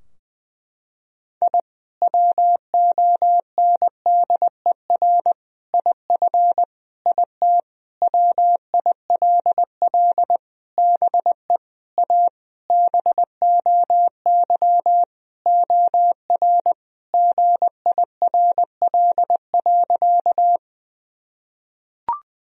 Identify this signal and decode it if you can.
Morse code, heard as a keyed tone, about 20 wpm, 700 Hz. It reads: I WONDER IF IT WILL BE A BOY OR GIRL.
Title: Sentences from Top 1000 Words - Review Part 3 - 20wpm